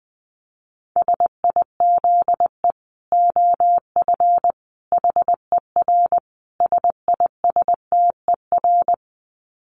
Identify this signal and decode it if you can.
Morse code, heard as a keyed tone, about 20 wpm, 700 Hz. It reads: SIZE OF HER SISTER
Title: Dit Challenge 20wpm